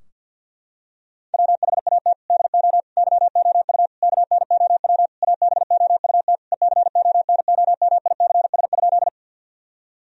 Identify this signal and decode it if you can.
Morse code, heard as a keyed tone, about 46 wpm, 700 Hz. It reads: WHAT DO YOU KNOW ABOUT ECONOMICS?
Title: Sentences from Top 1000 Words - Review Part 3 - 50wpm